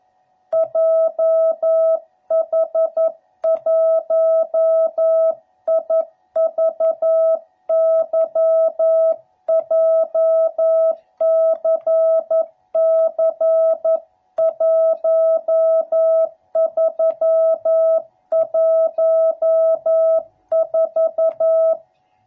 Morse code, heard as a keyed tone, about 11 wpm, 660 Hz. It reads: JH1IVY JCC1314
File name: jh8cbh / 2021cw / cw12b